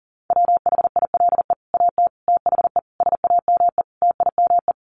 Morse code, heard as a keyed tone, about 40 wpm, 700 Hz. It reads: WHILE AT THE SAME TIME
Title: Sets of 5 Words 40wpm